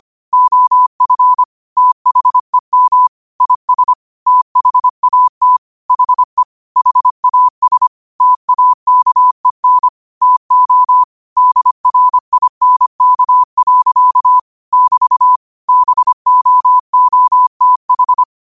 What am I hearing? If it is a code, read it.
Morse code, heard as a keyed tone, about 25 wpm, 1000 Hz. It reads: OF THEM IS THAT HE HAS TAKEN TO DRINK. <BT> BOOTH
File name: qotdfiles / 25wpm